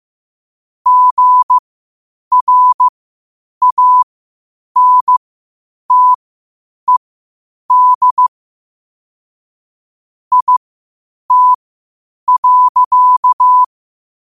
Morse code, heard as a keyed tone, about 15 wpm, spaced as for 8 wpm, 1000 Hz. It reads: GRANTED IT.
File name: qotdfiles / 8wpm